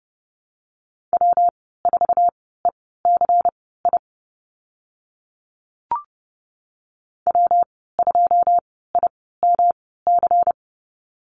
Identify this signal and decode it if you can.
Morse code, heard as a keyed tone, about 30 wpm, spaced as for 17 wpm, 700 Hz. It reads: W4ECI W2IMC
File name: Lesson 19g - Letter w - Call Signs 30wpm